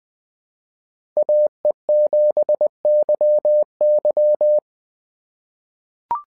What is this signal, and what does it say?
Morse code, heard as a keyed tone, about 20 wpm, 600 Hz. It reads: AE7YY
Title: SOTA QSO Practice 20wpm